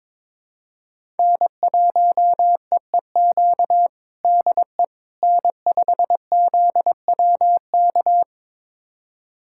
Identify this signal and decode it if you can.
Morse code, heard as a keyed tone, about 22 wpm, 700 Hz. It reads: N1EEQ DE N5ZWK
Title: USA Callsign DE Callsign - 5 Characters - 22wpm